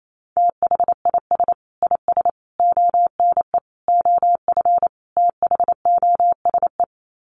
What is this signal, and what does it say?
Morse code, heard as a keyed tone, about 28 wpm, 700 Hz. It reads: THIS IS ONE OF THOSE
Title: Sets of 5 Words Encore2 28wpm